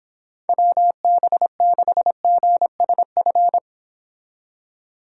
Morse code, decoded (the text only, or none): WB6GSF